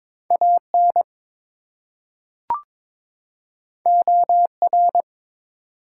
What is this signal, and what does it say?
Morse code, heard as a keyed tone, about 22 wpm, 700 Hz. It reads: AN OR
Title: Two Letter Words - 22wpm